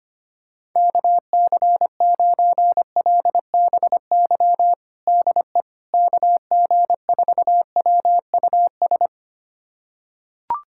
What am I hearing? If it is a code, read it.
Morse code, heard as a keyed tone, about 25 wpm, 700 Hz. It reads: KC9LBY DE KG4WUS
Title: USA Callsign DE Callsign - 6 Characters - 25wpm